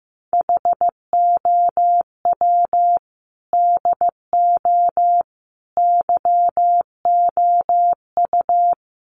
Morse code, heard as a keyed tone, about 15 wpm, 700 Hz. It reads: HOW DO YOU
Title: Sets of 4 Words 15wpm